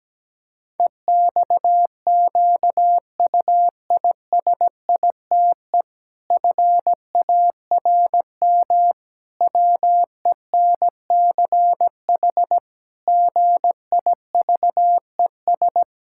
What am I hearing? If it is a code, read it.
Morse code, heard as a keyed tone, about 17 wpm, 700 Hz. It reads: EXQUISITE FARM WENCH GIVES
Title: Pangram Challenge v2 - 17wpm